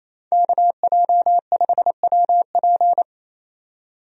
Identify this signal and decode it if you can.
Morse code, heard as a keyed tone, about 28 wpm, 700 Hz. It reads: KJ5WP
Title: USA Call Signs - 5 Characters - 28wpm